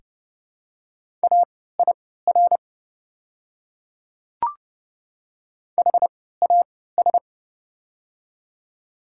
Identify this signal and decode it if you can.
Morse code, heard as a keyed tone, about 32 wpm, spaced as for 17 wpm, 700 Hz. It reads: AIR HAS
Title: 3x Character Spacing - Top 300 Words - 30wpm